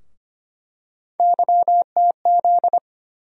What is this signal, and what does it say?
Morse code, heard as a keyed tone, about 25 wpm, 700 Hz. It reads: YTZ
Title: USA Call Sign Postfixes - 3 Letters 25wpm